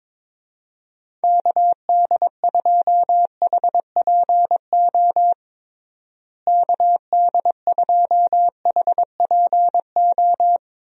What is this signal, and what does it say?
Morse code, heard as a keyed tone, about 22 wpm, 700 Hz. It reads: KD2HPO KD2HPO
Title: USA Call Signs - 6 Characters - Repeated 2x - 22wpm